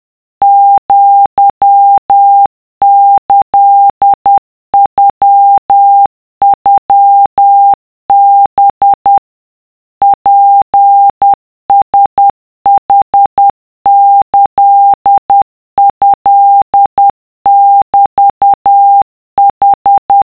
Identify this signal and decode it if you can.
Morse code (Japanese Wabun code), heard as a keyed tone, about 10 wpm, 800 Hz. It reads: アキノノハ ツラヌキトメヌ
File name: 文屋康秀